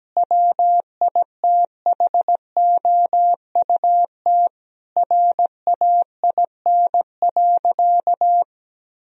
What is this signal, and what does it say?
Morse code, heard as a keyed tone, about 17 wpm, 700 Hz. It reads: WITHOUT RAIN.
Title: Sentences from Top 1000 Words - Review Part 3 - 17wpm